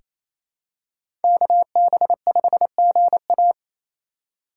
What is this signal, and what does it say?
Morse code, heard as a keyed tone, about 28 wpm, 700 Hz. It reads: KB5GA
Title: K1USN SST QSO Practice - 28wpm